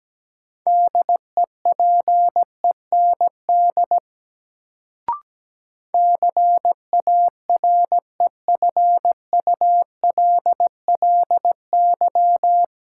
Morse code, heard as a keyed tone, about 17 wpm, 700 Hz. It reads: DEPEND CAREFULLY
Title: Sentences from Top 1300 Words - New Word Drill Alt - 17wpm